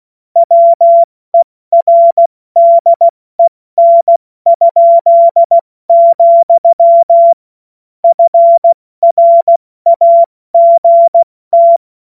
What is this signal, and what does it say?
Morse code, heard as a keyed tone, about 16 wpm, spaced as for 14 wpm, 680 Hz. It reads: WERDEN?, FRAGT
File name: Morsekurs / Witze / Witz 27. Juni 2023 70 F80